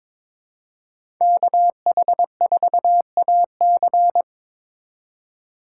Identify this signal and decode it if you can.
Morse code, heard as a keyed tone, about 21 wpm, 700 Hz. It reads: KH4AC
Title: USA Call Signs - 4x Character Spacing - 22wpm